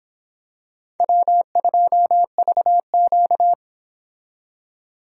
Morse code, heard as a keyed tone, about 26 wpm, 700 Hz. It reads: W2VQ